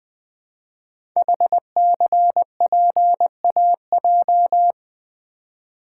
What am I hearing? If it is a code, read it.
Morse code, heard as a keyed tone, about 20 wpm, 700 Hz. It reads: HCPAJ
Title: Code Groups - 5 Letters - 20wpm